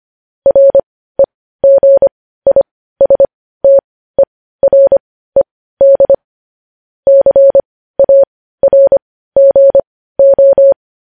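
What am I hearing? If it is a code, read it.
Morse code, heard as a keyed tone, about 25 wpm, spaced as for 15 wpm, 550 Hz. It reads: REGISTERED CARGO
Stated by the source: Morse code news headlines sent at 15 words per minute.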